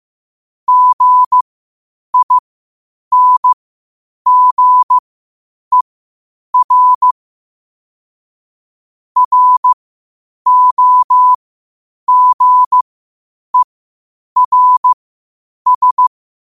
Morse code, heard as a keyed tone, about 15 wpm, spaced as for 8 wpm, 1000 Hz. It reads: GINGER ROGERS
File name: qotdfiles / 8wpm